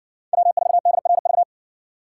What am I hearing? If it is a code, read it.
Morse code, heard as a keyed tone, about 60 wpm, 700 Hz. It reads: W4DRV